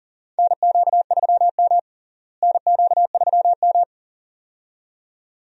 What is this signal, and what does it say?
Morse code, heard as a keyed tone, about 40 wpm, 700 Hz. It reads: NQ3M NQ3M
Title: USA Call Signs - 4 Characters - Repeated 2x - 40wpm